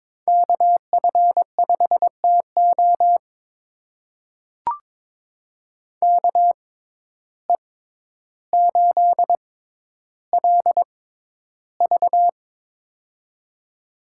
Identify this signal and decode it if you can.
Morse code, heard as a keyed tone, about 22 wpm, spaced as for 11 wpm, 700 Hz. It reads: KF5TO KE8LV
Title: USA Call Signs - 5 Characters - 6x Character Spacing - 22wpm